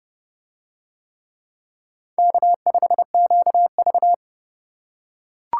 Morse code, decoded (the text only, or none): K5QV